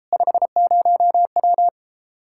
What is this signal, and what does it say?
Morse code, heard as a keyed tone, about 33 wpm, 700 Hz. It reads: 50W